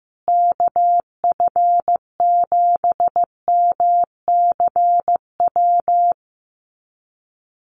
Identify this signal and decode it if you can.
Morse code, heard as a keyed tone, about 14 wpm, 700 Hz. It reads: KF7MCW
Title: USA Call Signs Repeated 2x 15wpm